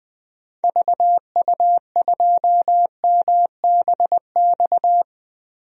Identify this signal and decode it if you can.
Morse code, heard as a keyed tone, about 20 wpm, 700 Hz. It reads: VU2MBX
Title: DXCC-100 Call Signs - 6 Characters - 20wpm